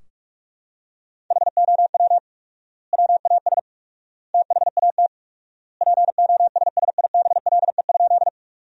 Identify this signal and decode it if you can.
Morse code, heard as a keyed tone, about 46 wpm, 700 Hz. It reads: HOW WAS THAT POSSIBLE?
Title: 4x Word-Spacing - Sentences from Top 500 Words - 45wpm